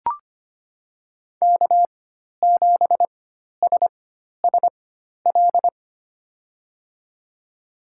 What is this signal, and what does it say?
Morse code, heard as a keyed tone, about 25 wpm, spaced as for 12 wpm, 700 Hz. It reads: K7SSL
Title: USA Call Signs - 5 Characters - 4x Character Spacing - 25wpm